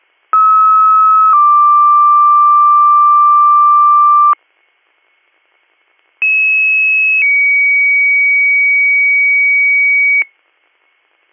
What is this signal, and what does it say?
Two-tone page, heard as a one-second tone, then a three-second tone, about 1290 then 1180 Hz, about 2600 then 2250 Hz. It is heard from 0.3 to 10.2 s.